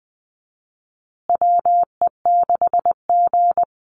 Morse code, heard as a keyed tone, about 19 wpm, 700 Hz. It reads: WE6G
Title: USA Call Signs Repeated 3x 20wpm